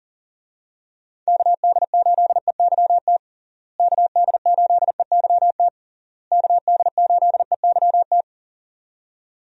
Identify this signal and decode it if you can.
Morse code, heard as a keyed tone, about 37 wpm, 700 Hz. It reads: KD8EYT KD8EYT KD8EYT
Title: USA Call Signs - 6 Characters - Repeated 3x - 40wpm